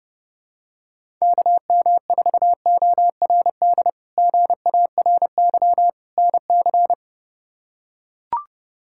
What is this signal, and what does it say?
Morse code, heard as a keyed tone, about 31 wpm, 700 Hz. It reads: KM4ORD GARY NC
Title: K1USN SST QSO Practice - 30wpm